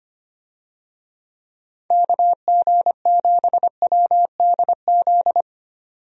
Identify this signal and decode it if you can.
Morse code, heard as a keyed tone, about 25 wpm, 700 Hz. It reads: KG7WDZ
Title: USA Call Signs - 6 Characters - 25wpm